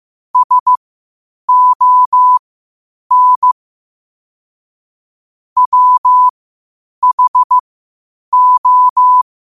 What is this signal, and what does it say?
Morse code, heard as a keyed tone, about 15 wpm, spaced as for 8 wpm, 1000 Hz. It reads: SON WHO
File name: qotdfiles / 8wpm